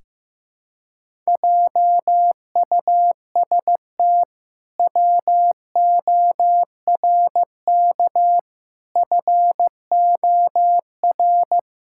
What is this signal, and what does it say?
Morse code, heard as a keyed tone, about 15 wpm, 700 Hz. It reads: JUST WORK FOR
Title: Pangram Challenge v2 - 15wpm